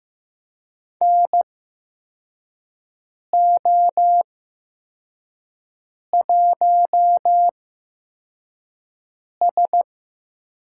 Morse code, heard as a keyed tone, about 15 wpm, spaced as for 4 wpm, 700 Hz. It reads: NO1S
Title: USA Call Signs - 4 Characters - 8x Character Spacing - 15wpm